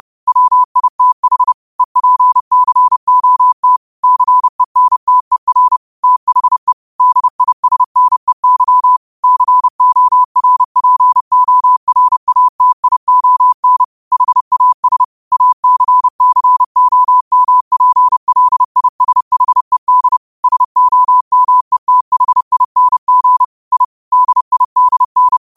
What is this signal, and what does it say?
Morse code, heard as a keyed tone, about 30 wpm, 1000 Hz. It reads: WITH EPCOT CENTER THE DISNEY CORPORATION HAS ACCOMPLISHED SOMETHING I DIDN